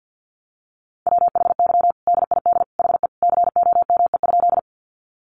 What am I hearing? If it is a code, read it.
Morse code, heard as a keyed tone, about 50 wpm, 700 Hz. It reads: WHY DID HE COME?
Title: Sentences from Top 200 Words 50wpm